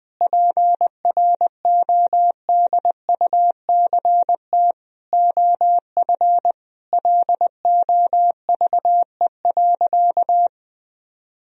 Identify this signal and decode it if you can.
Morse code, heard as a keyed tone, about 21 wpm, 700 Hz. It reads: PRODUCT OF LOVE.
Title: Sentences from Top 1000 Words - Review Part 2 - 20wpm